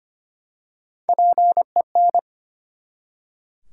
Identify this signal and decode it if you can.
Morse code, heard as a keyed tone, about 25 wpm, 700 Hz. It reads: PEN
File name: practice / 25wpm / Easy Words with Hints - 25wpm